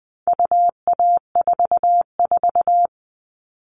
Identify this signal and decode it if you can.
Morse code, heard as a keyed tone, about 20 wpm, 700 Hz. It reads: UA44
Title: All Cut Numbers - 4-Digit 20wpm